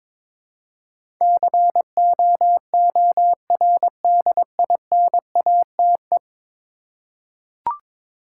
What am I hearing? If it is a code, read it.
Morse code, heard as a keyed tone, about 22 wpm, 700 Hz. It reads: COORDINATE
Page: morse-code-ninja-practice-22wpm